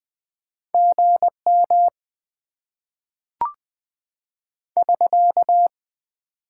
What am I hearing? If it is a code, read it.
Morse code, heard as a keyed tone, about 20 wpm, 700 Hz. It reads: GM <SK>
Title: Top 90 QSO Elements - 6x Character Spacing - 20wpm